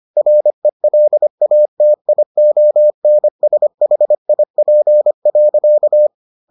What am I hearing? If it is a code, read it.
Morse code, heard as a keyed tone, about 25 wpm, 600 Hz. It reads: RELATIONSHIP.